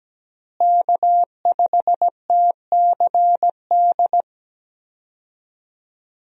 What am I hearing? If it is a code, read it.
Morse code, heard as a keyed tone, about 17 wpm, 700 Hz. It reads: K5TCD
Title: USA Call Signs - 5 Characters - Repeated 2x - 17wpm